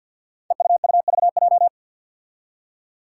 Morse code, heard as a keyed tone, about 50 wpm, 700 Hz. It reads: EUUVJ
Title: Five Challenging Letter-Number Combinations 50wpm